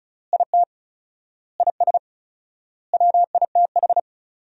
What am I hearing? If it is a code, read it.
Morse code, heard as a keyed tone, about 36 wpm, 700 Hz. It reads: IT IS WITH
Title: 4x Word-Spacing - Sentences from Top 100 Words - 35wpm